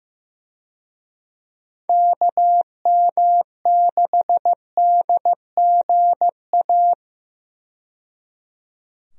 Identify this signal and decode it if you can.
Morse code, heard as a keyed tone, about 15 wpm, 700 Hz. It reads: KM6DGA
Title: USA Call Signs - 6 Characters - 15wpm